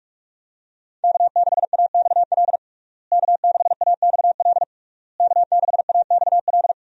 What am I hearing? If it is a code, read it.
Morse code, heard as a keyed tone, about 43 wpm, 700 Hz. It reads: K6AXL K6AXL K6AXL
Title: USA Call Signs - 5 Characters - Repeated 3x - 45wpm